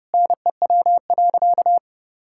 Morse code, heard as a keyed tone, about 30 wpm, 700 Hz. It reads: NEW.
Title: Sentences from Top 1600 Words 30wpm